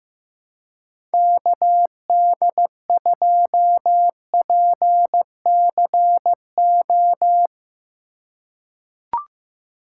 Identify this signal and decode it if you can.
Morse code, heard as a keyed tone, about 15 wpm, 700 Hz. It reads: KD2PCO
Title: USA Call Signs - 3x Character Spacing - 15wpm